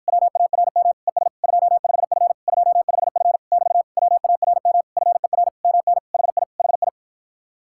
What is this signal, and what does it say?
Morse code, heard as a keyed tone, about 54 wpm, 700 Hz. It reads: WARM ES 25F 25F <BT> WARM FER MN HI HI